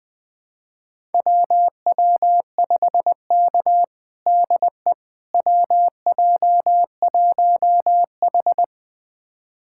Morse code, heard as a keyed tone, about 20 wpm, 700 Hz. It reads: WW5K DE WJ1H
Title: USA Callsign DE Callsign - 4 Characters - 20wpm